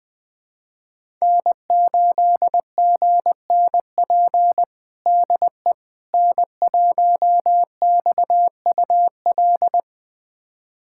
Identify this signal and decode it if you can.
Morse code, heard as a keyed tone, about 20 wpm, 700 Hz. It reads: N8GNP DE N1XUL
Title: USA Callsign DE Callsign - 5 Characters - 20wpm